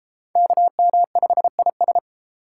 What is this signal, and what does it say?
Morse code, heard as a keyed tone, about 30 wpm, 700 Hz. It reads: KM5IS